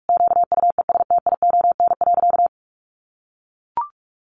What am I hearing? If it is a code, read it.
Morse code, heard as a keyed tone, about 44 wpm, 700 Hz. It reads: QUESTION.
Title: Sentences from Top 1200 Words 45wpm